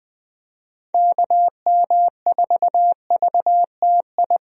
Morse code, heard as a keyed tone, about 21 wpm, 700 Hz. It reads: KM4VTI
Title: USA Call Signs - 6 Characters - Repeated 2x - 20wpm